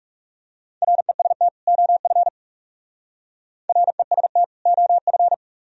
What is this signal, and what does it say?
Morse code, heard as a keyed tone, about 38 wpm, 700 Hz. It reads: REST OF REST OF